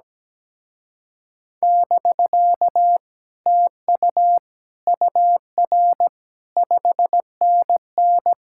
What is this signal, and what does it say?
Morse code, heard as a keyed tone, about 17 wpm, 700 Hz. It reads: <BK> TU UR 5NN
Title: Speed-Racing Common QSO Phrases - 17wpm